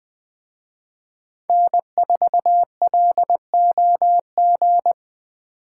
Morse code, heard as a keyed tone, about 20 wpm, 700 Hz. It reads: N4LOG